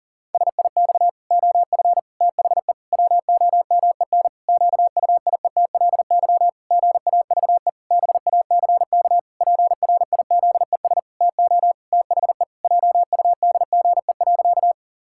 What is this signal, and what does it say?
Morse code, heard as a keyed tone, about 40 wpm, 700 Hz. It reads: SIX OF THE WOMEN QUIETLY GAVE BACK PRIZES TO THE JUDGE.
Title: Pangram Challenge v2 - 40wpm